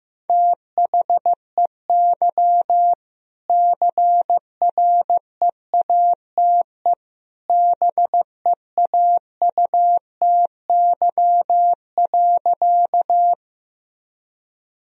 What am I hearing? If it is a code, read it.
Morse code, heard as a keyed tone, about 16 wpm, 700 Hz. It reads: THEY CREATE BEAUTY.
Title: Sentences from Top 1000 Words - Review Part 3 - 15wpm